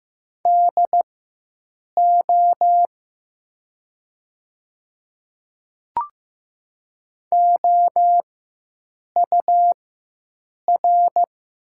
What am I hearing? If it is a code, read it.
Morse code, heard as a keyed tone, about 15 wpm, spaced as for 7 wpm, 700 Hz. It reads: DO OUR
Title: 4x Character Spacing - Top 300 Words - 15wpm